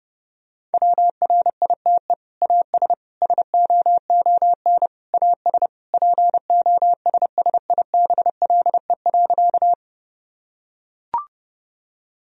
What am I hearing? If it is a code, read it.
Morse code, heard as a keyed tone, about 30 wpm, 700 Hz. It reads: WRITE AS SOON AS POSSIBLE.
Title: Sentences from Top 1000 Words - Review Part 1 - 30wpm